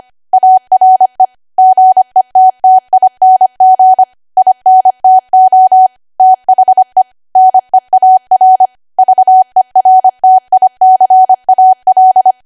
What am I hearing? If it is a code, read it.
Morse code, heard as a keyed tone, about 25 wpm, 750 Hz. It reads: ARE GETTING INTO THE NEAR VERTICAL
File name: Archive / 25 WPM / 25WPM